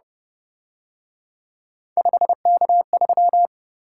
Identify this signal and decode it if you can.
Morse code, heard as a keyed tone, about 28 wpm, 700 Hz. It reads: 5K3